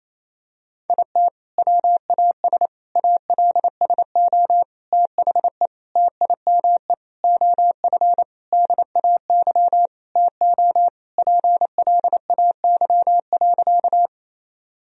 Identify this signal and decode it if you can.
Morse code, heard as a keyed tone, about 28 wpm, 700 Hz. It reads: IT WAS ALSO THE TIME OF DAY TO PLAY.
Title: Sentences from Top 1000 Words - Review Part 1 - 28wpm